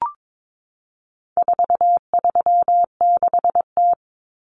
Morse code, heard as a keyed tone, about 22 wpm, 700 Hz. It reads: 436T
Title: Common Cut Numbers - 4-Digit 22wpm